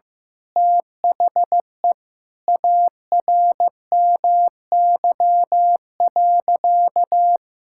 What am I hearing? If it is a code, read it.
Morse code, heard as a keyed tone, about 15 wpm, 700 Hz. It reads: THE ARMY.
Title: Sentences from Top 1000 Words - Review Part 6 - 15wpm